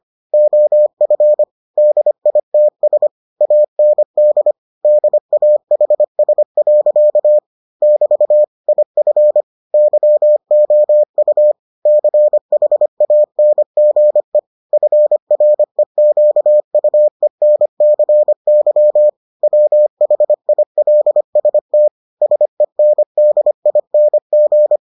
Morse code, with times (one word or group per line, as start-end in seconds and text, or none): 0.3-1.4 OF
1.8-3.1 DITS
3.4-4.5 AND
4.8-7.4 DAHS.
7.8-8.4 <BT>
8.7-9.4 IF
9.7-11.5 YOU
11.9-14.4 CHANGE
14.7-19.1 FREQUENCY
19.4-21.9 WHILST
22.2-24.8 SENDING